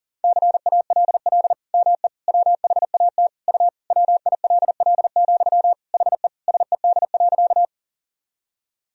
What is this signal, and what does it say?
Morse code, heard as a keyed tone, about 40 wpm, 700 Hz. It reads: CALL ME WHAT U WILL, HE SED.
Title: Sentences from Top 200 Words and Abbreviations 40wpm